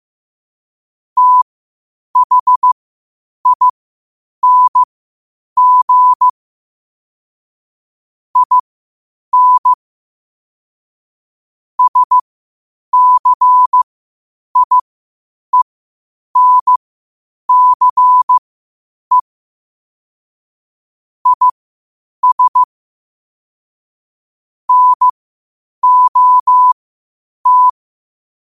Morse code, heard as a keyed tone, about 15 wpm, spaced as for 8 wpm, 1000 Hz. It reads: THING IN SCIENCE IS NOT
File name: qotdfiles / 8wpm